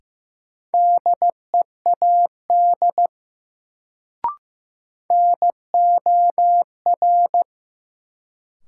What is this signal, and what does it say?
Morse code, heard as a keyed tone, about 15 wpm, 700 Hz. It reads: DEAD NOR
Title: Sentences from Top 1500 Words - New Word Drill Alt - 15wpm